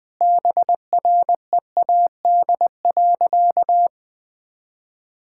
Morse code, heard as a keyed tone, about 20 wpm, 700 Hz. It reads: BREAD.
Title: Sentences from Top 1600 Words 20wpm